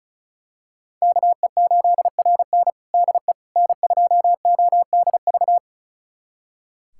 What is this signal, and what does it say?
Morse code, heard as a keyed tone, about 36 wpm, 700 Hz. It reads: KE8RN DE N2ODV
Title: USA Callsign DE Callsign - 5 Characters - 35wpm